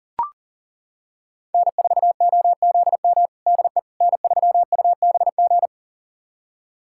Morse code, heard as a keyed tone, about 40 wpm, 700 Hz. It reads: N4OZM DE N3UBG
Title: USA Callsign DE Callsign - 5 Characters - 40wpm